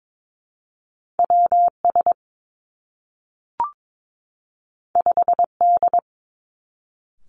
Morse code, heard as a keyed tone, about 22 wpm, 700 Hz. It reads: W S 5 D
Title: Two Challenging Letter-Number Combinations 22wpm